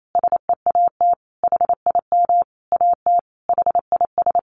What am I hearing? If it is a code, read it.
Morse code, heard as a keyed tone, about 28 wpm, 700 Hz. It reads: SEAT HIM AT HIS
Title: Sentences from Top 1500 Words 28wpm